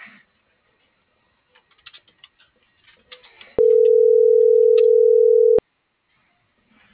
Ringback tone (North America), 440 + 480 Hz.
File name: CENTRAL MABE / out noemi.gutierrez